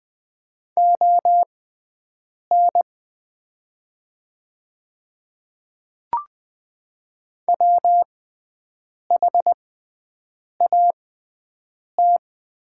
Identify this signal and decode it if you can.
Morse code, heard as a keyed tone, about 20 wpm, spaced as for 7 wpm, 700 Hz. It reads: ON WHAT
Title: 6x Character Spacing - Top 300 Words - 20wpm